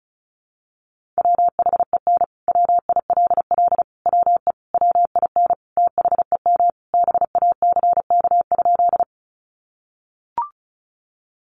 Morse code, heard as a keyed tone, about 34 wpm, 700 Hz. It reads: WHEN WILL WE WIN THEM BACK?
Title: Sentences from Top 1500 Words 35wpm